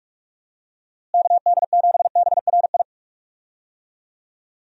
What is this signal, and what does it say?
Morse code, heard as a keyed tone, about 46 wpm, 700 Hz. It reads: KD7BRI
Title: USA Call Signs - 6 Characters - Repeated 3x - 45wpm